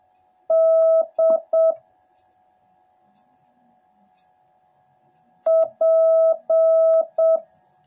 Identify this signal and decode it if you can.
Morse code, heard as a keyed tone, about 7 wpm, 650 Hz. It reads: D P